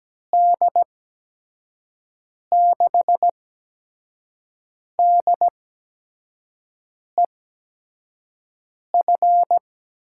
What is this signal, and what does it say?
Morse code, heard as a keyed tone, about 17 wpm, spaced as for 5 wpm, 700 Hz. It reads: D6DEF
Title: USA Call Signs - 6 Characters - 8x Character Spacing - 17wpm